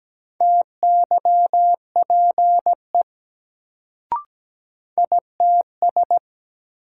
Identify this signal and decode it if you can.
Morse code, heard as a keyed tone, about 17 wpm, 700 Hz. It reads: TYPE ITS
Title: Sentences from Top 600 Words - New Word Drill - 17wpm